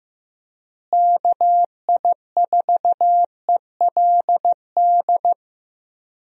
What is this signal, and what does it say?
Morse code, heard as a keyed tone, about 15 wpm, 700 Hz. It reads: KI4ELD
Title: USA Call Signs - 6 Characters - Repeated 3x - 15wpm